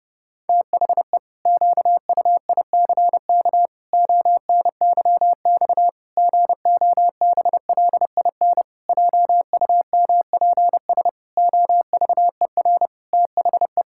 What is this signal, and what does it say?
Morse code, heard as a keyed tone, about 30 wpm, 700 Hz. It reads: THE QUICK ONYX GOBLIN JUMPS OVER THE
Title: Pangram Challenge v2 - 30wpm